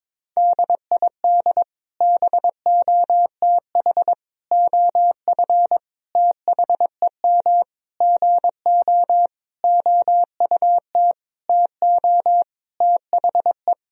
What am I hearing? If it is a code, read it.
Morse code, heard as a keyed tone, about 22 wpm, 700 Hz. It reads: DID BOTH OF THEM GO OUT TO THE
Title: Sentences from Top 1000 Words - Review Part 2 - 22wpm